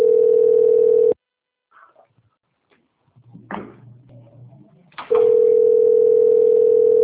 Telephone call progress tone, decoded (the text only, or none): ringback tone